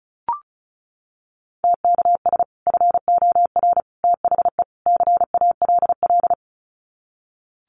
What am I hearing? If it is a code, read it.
Morse code, heard as a keyed tone, about 34 wpm, 700 Hz. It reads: TKS FOR THE CALL
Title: Common QSO Phrases 35wpm